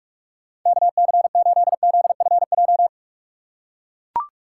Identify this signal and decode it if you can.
Morse code, heard as a keyed tone, about 43 wpm, 700 Hz. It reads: KC8ZFJ
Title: USA Call Signs - 6 Characters - 6x Character Spacing - 45wpm